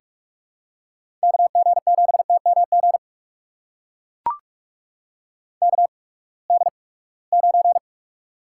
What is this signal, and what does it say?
Morse code, heard as a keyed tone, about 42 wpm, spaced as for 25 wpm, 700 Hz. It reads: KG7TMG KD9
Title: USA Call Signs - 8x Character Spacing - 45wpm